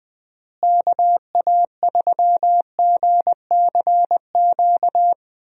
Morse code, heard as a keyed tone, about 21 wpm, 700 Hz. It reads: KA3GCQ
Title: USA Call Signs - 6 Characters - Repeated 3x - 20wpm